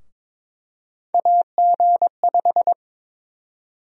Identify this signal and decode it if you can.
Morse code, heard as a keyed tone, about 21 wpm, 700 Hz. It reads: AG5
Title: USA Call Sign Prefixes - Repeated 2x - 22wpm